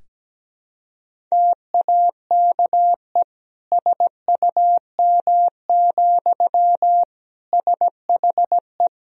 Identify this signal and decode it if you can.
Morse code, heard as a keyed tone, about 17 wpm, 700 Hz. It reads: TAKE SUM, SHE
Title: Sentences from Top 200 Words and Abbreviations 17wpm